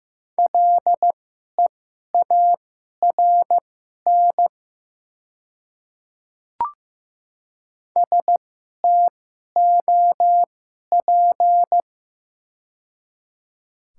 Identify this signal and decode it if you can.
Morse code, heard as a keyed tone, about 15 wpm, spaced as for 11 wpm, 690 Hz. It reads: LEARN STOP
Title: 2x Character Spacing - Top 500 Words - 15wpm